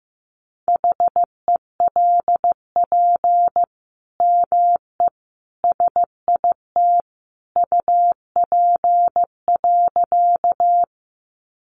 Morse code, heard as a keyed tone, about 15 wpm, 700 Hz. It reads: HELP ME SIT UP.
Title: Sentences from Top 500 Words 15wpm